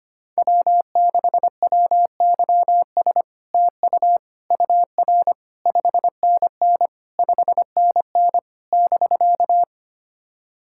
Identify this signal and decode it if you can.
Morse code, heard as a keyed tone, about 25 wpm, 700 Hz. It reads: W6WYS TU UR 5NN 5NN <BK>
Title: SOTA QSO Practice 25wpm